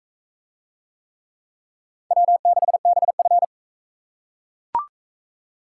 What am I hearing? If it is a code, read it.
Morse code, heard as a keyed tone, about 38 wpm, 700 Hz. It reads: W6BF